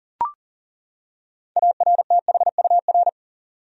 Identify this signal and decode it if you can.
Morse code, heard as a keyed tone, about 40 wpm, 700 Hz. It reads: ARTHUR
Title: USA Common Names - 40wpm